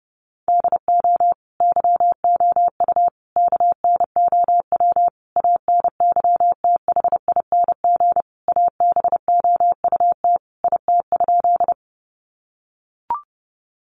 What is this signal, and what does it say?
Morse code, heard as a keyed tone, about 30 wpm, 700 Hz. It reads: DO YOU KNOW ANYTHING ABOUT IT?
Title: Sentences from Top 1000 Words 30wpm